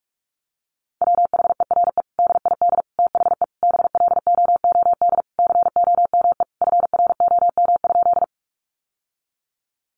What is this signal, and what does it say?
Morse code, heard as a keyed tone, about 44 wpm, 700 Hz. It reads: WHERE DID THE BLOOD COME FROM?
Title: Sentences from Top 1100 Words 45wpm